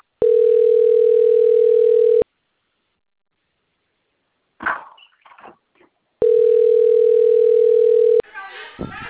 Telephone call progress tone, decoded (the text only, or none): ringback tone